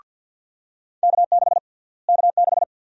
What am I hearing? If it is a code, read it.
Morse code, heard as a keyed tone, about 51 wpm, 700 Hz. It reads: K6 K6
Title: USA Call Sign Prefixes - Repeated 2x - 50wpm